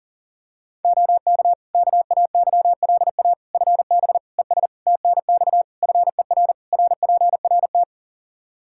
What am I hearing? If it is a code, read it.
Morse code, heard as a keyed tone, about 40 wpm, 700 Hz. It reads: OK KAYLA FB ES TNX FER RPRT
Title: K1USN SST QSO Practice 40wpm